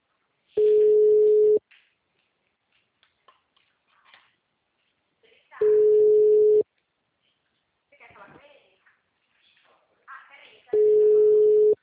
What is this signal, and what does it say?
Ringback tone (Europe, 425 Hz), with speech.
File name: out- -2006- - - .58